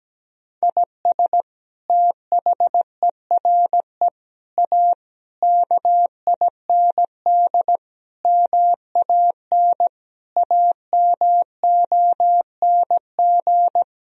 Morse code, heard as a keyed tone, about 17 wpm, 700 Hz. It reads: IS THERE A KIND MAN AMONG
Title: Sentences from Top 1000 Words - Review Part 3 - 17wpm